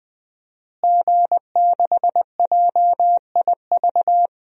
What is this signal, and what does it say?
Morse code, heard as a keyed tone, about 20 wpm, 700 Hz. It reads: G6JIV
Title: DXCC-100 Call Signs - 5 Characters - Repeated 3x - 20wpm